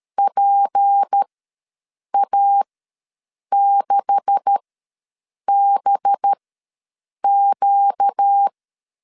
Morse code, heard as a keyed tone, about 13 wpm, spaced as for 7 wpm, 800 Hz. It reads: PA6BQ
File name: LEZIONI / 9 2867 ws 12wpm 3431